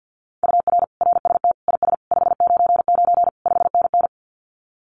Morse code, heard as a keyed tone, about 48 wpm, 700 Hz. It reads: UR RST IS 599 5NN
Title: Common QSO Phrases Encore 50wpm